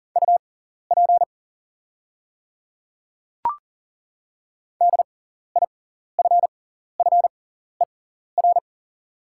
Morse code, heard as a keyed tone, about 39 wpm, spaced as for 14 wpm, 690 Hz. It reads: UP DIFFER
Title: 6x Character Spacing - Top 300 Words - 40wpm